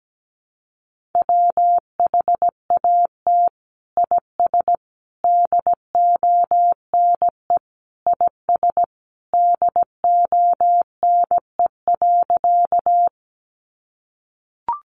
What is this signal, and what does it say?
Morse code, heard as a keyed tone, about 17 wpm, 700 Hz. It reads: WHAT IS DONE IS DONE.
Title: Sentences from Top 500 Words 17wpm